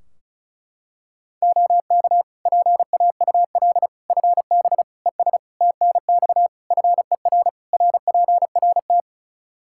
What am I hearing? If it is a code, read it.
Morse code, heard as a keyed tone, about 36 wpm, 700 Hz. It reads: OK PAUL FB ES TNX FER RPRT
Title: K1USN SST QSO Practice 35wpm